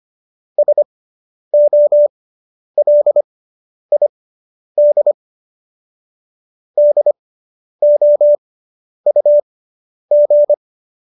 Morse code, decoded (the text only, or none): SOLID DOUG